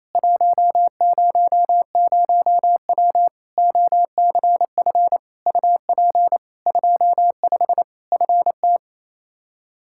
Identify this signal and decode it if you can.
Morse code, heard as a keyed tone, about 28 wpm, 700 Hz. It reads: 100W OCF UP 25 FT
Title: Practice QSO - Exchange 2 - Info - 28wpm